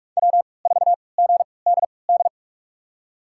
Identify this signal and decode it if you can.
Morse code, heard as a keyed tone, about 46 wpm, spaced as for 26 wpm, 700 Hz. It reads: W4GDD
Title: USA Call Signs - 5 Characters - 3x Character Spacing - 45wpm